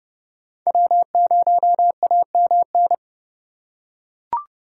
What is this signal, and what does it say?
Morse code, heard as a keyed tone, about 29 wpm, 700 Hz. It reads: W0AMN
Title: USA Call Signs - 8x Character Spacing - 30wpm